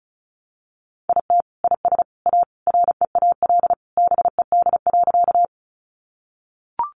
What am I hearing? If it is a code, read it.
Morse code, heard as a keyed tone, about 34 wpm, 700 Hz. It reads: IT IS A REAL BED.
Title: Sentences from Top 500 Words 35wpm